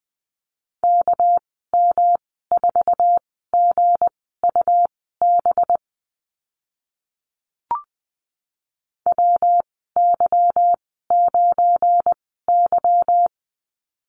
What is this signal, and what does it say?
Morse code, heard as a keyed tone, about 20 wpm, spaced as for 14 wpm, 700 Hz. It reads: KM4GUB WY9Y
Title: 207 Lesson 33g - Letter j - Call Signs 20wpm